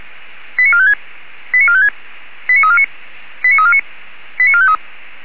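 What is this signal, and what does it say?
Five-tone selective call (ZVEI), heard as quick runs of five tones, tones of about 70 ms. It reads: 89456 89456 89349 89349 89453
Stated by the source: ZVEI five-tone selective call received by SDR.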